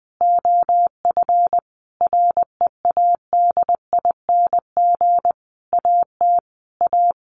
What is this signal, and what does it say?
Morse code, heard as a keyed tone, about 20 wpm, 700 Hz. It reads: OF READING AT A
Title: Sentences from Top 900 Words 20wpm